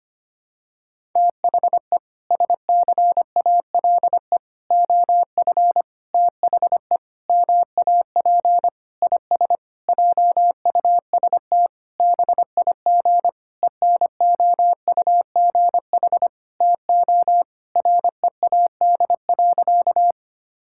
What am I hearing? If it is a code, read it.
Morse code, heard as a keyed tone, about 25 wpm, 700 Hz. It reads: THE SCALE OF THE MAP IS JUST BIG ENOUGH TO READ.
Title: Sentences from Top 1000 Words - Review Part 5 - 25wpm